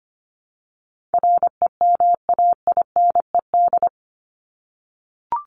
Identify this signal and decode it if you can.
Morse code, heard as a keyed tone, about 25 wpm, 700 Hz. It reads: REMAINED
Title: ED Suffix - Top 100 Words with Base Words 25wpm